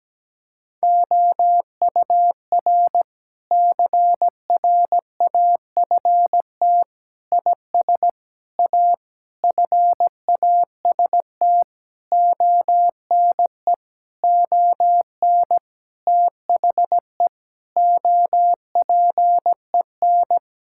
Morse code, heard as a keyed tone, about 17 wpm, 700 Hz. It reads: OUR CRAFT IS A FAST ONE ON THE OPEN
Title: Sentences from Top 1000 Words - Review Part 6 - 17wpm